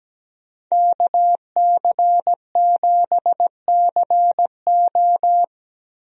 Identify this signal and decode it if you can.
Morse code, heard as a keyed tone, about 17 wpm, 700 Hz. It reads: KC7CO
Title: USA Call Signs - 5 Characters - Repeated 2x - 17wpm